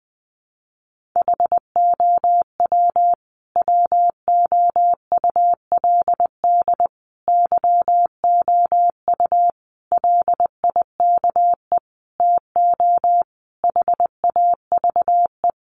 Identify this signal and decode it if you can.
Morse code, heard as a keyed tone, about 20 wpm, 700 Hz. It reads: HOW WOULD YOU LIKE TO HAVE
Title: Sentences from Top 100 Words 20wpm